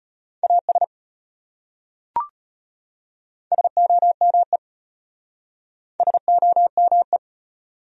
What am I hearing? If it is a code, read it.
Morse code, heard as a keyed tone, about 33 wpm, 700 Hz. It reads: AS SOME SOME